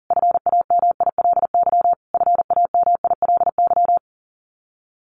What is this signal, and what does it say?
Morse code, heard as a keyed tone, about 37 wpm, 700 Hz. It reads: FAMILY FAMILY
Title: Top 500 Words Repeated 40wpm